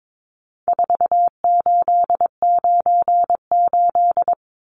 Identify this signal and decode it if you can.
Morse code, heard as a keyed tone, about 22 wpm, 700 Hz. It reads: 4898
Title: Numbers - 4-Digit 22wpm